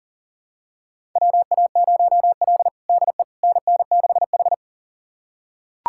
Morse code, heard as a keyed tone, about 38 wpm, 700 Hz. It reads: WA0L DE NN6H
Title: USA Callsign DE Callsign - 4 Characters - 40wpm